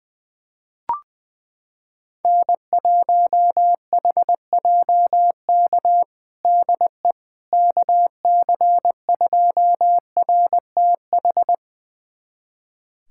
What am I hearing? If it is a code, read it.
Morse code, heard as a keyed tone, about 20 wpm, 700 Hz. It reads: N1HJK DE KC2RTH
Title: USA Callsign DE Callsign 20wpm